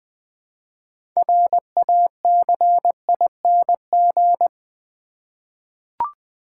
Morse code, heard as a keyed tone, about 20 wpm, 700 Hz. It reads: RACING
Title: 6 Letter Words - First 3 Letters Given - 20wpm